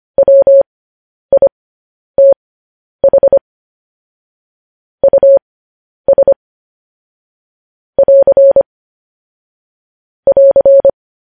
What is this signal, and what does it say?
Morse code, heard as a keyed tone, about 25 wpm, spaced as for 10 wpm, 550 Hz. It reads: WITH US <AR> <AR>
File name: podcast / News headlines-1x-25x10